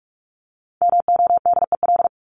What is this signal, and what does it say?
Morse code, heard as a keyed tone, about 44 wpm, 700 Hz. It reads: MODEL
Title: Top 501-750 Words 45wpm